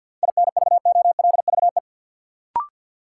Morse code, heard as a keyed tone, about 53 wpm, 700 Hz. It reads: INVOLVE
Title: Sentences from Top 1200 Words - New Word Drill Alt - 50wpm